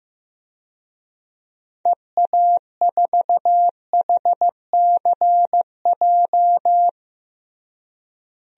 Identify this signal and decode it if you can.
Morse code, heard as a keyed tone, about 15 wpm, 700 Hz. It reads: EA4HCJ
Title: DXCC-100 Call Signs - 6 Characters - Repeated 2x - 15wpm